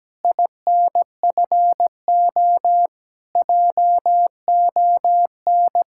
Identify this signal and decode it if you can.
Morse code, heard as a keyed tone, about 17 wpm, 700 Hz. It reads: INFO JON
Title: Speed-Racing Common QSO Phrases - 17wpm